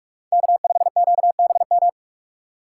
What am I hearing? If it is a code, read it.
Morse code, heard as a keyed tone, about 46 wpm, 690 Hz. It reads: K5QBM
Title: USA Call Signs - 5 Characters - 2x Character Spacing - 45wpm